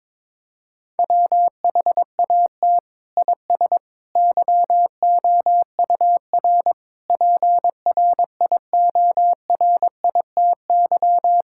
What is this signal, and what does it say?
Morse code, heard as a keyed tone, about 22 wpm, 700 Hz. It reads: WHAT IS YOUR PRIORITY
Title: Sentences from Top 1600 Words 22wpm